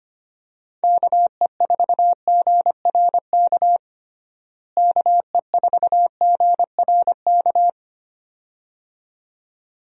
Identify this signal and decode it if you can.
Morse code, heard as a keyed tone, about 25 wpm, 700 Hz. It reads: KE4GRK KE4GRK
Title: USA Call Signs - 6 Characters - Repeated 2x - 25wpm